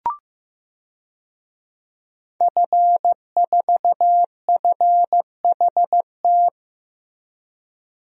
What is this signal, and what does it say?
Morse code, heard as a keyed tone, about 15 wpm, 700 Hz. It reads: F4FHT
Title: DXCC-100 Call Signs - 5 Characters - Repeated 2x - 15wpm